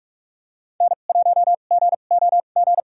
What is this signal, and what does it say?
Morse code, heard as a keyed tone, about 45 wpm, spaced as for 32 wpm, 690 Hz. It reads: N1GOG
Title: USA Call Signs - 5 Characters - 2x Character Spacing - 45wpm